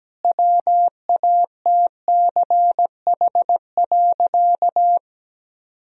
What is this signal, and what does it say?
Morse code, heard as a keyed tone, about 17 wpm, 700 Hz. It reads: WATCH.
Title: Sentences from Top 1000 Words - Review Part 1 - 17wpm